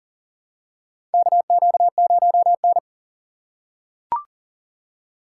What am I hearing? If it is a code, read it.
Morse code, heard as a keyed tone, about 40 wpm, 700 Hz. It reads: KQ0N
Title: USA Call Signs - 4 Characters - 3x Character Spacing - 40wpm